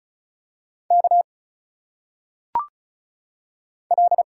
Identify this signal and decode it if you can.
Morse code, heard as a keyed tone, about 34 wpm, 700 Hz. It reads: K L
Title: Single Letters - Phonetic Pronunciation 35wpm